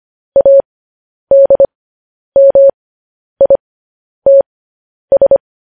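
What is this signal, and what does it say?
Morse code, heard as a keyed tone, about 25 wpm, spaced as for 10 wpm, 550 Hz. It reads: ADMITS
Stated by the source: Morse code sending of news headlines, speed 10.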